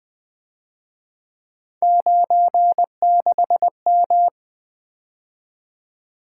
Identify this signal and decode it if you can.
Morse code, heard as a keyed tone, about 23 wpm, 700 Hz. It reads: 96M
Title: Speed-Racing - Three Letter-Number Combinations - 20wpm